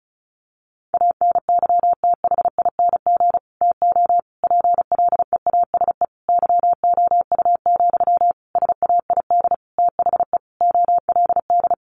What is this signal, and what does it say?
Morse code, heard as a keyed tone, about 34 wpm, 700 Hz. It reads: ANYTHING TO PLEASE YOU, SAID THE OLD
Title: Sentences from Top 1500 Words 35wpm